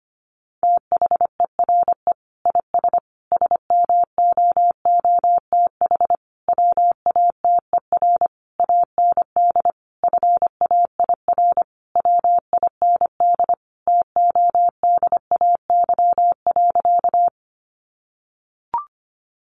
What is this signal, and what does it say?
Morse code, heard as a keyed tone, about 25 wpm, 700 Hz. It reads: THERE IS SMOOTH WATER AND FAIR WIND TODAY.
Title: Sentences from Top 1400 Words 25wpm